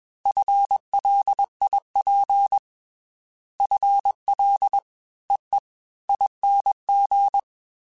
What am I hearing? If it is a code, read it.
Morse code, heard as a keyed tone, about 21 wpm, 800 Hz. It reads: FLIP FL EE ING
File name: FL / FL mp3s / Fast FL @20